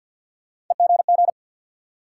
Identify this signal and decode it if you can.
Morse code, heard as a keyed tone, about 47 wpm, 700 Hz. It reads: EGG
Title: Sentences from Top 1000 Words - Review Part 7 - New Word Drill - 50wpm